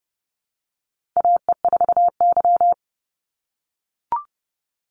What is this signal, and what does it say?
Morse code, heard as a keyed tone, about 30 wpm, 700 Hz. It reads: AE4Y
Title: NA SOTA Chaser Callsigns 30wpm